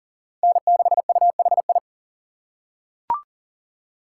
Morse code, heard as a keyed tone, about 40 wpm, 700 Hz. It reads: N6UHI
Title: USA Call Signs - 5 Characters - 6x Character Spacing - 40wpm